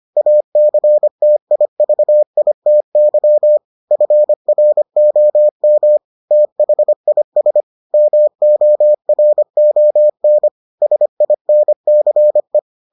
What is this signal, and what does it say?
Morse code, heard as a keyed tone, about 25 wpm, 600 Hz. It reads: ACTIVITY FROM THIS MORON SINCE